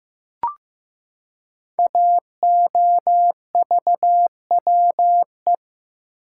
Morse code, heard as a keyed tone, about 15 wpm, 700 Hz. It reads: AOVWE
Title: Code Groups - 5 Letters - 15wpm